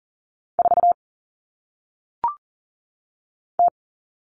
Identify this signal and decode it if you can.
Morse code, heard as a keyed tone, about 39 wpm, 700 Hz. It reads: 4 T
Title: All Cut Numbers - 1-Digit 40wpm